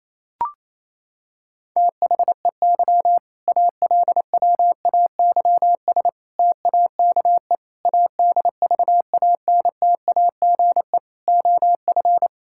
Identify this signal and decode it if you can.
Morse code, heard as a keyed tone, about 28 wpm, 700 Hz. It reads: THEY ALWAYS TAKE ADVANTAGE OF
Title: Sentences from Top 1000 Words - Review Part 6 - 28wpm